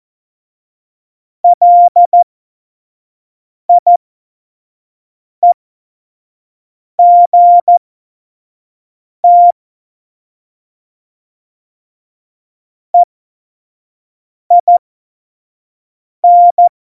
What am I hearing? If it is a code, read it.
Morse code, heard as a keyed tone, about 14 wpm, spaced as for 5 wpm, 700 Hz. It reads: LIEGT EIN
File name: Morsekurs / Witze / Witz 25 Juli 17, Tempo 25 F70